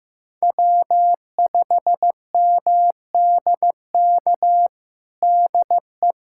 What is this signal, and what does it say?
Morse code, heard as a keyed tone, about 15 wpm, 700 Hz. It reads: W5MDK DE
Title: USA Callsign DE Callsign - 5 Characters - 15wpm